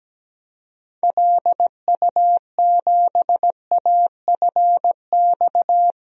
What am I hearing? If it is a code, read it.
Morse code, heard as a keyed tone, about 17 wpm, 700 Hz. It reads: LU7AFX
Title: DXCC-100 Call Signs - 6 Characters - 17wpm